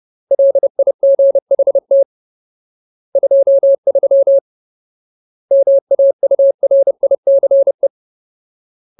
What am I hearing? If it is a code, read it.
Morse code, heard as a keyed tone, about 30 wpm, 550 Hz. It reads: LIGHT 23 MAURICE
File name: Ultimate Word List / 30WPM Ultimate